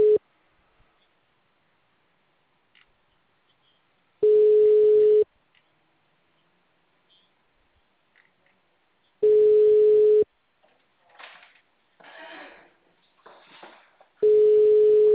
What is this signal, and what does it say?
Ringback tone (Europe, 425 Hz).